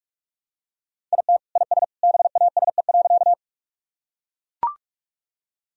Morse code, heard as a keyed tone, about 42 wpm, 700 Hz. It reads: IT IS BASE.
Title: Sentences from Top 1000 Words - Review Part 1 - 45wpm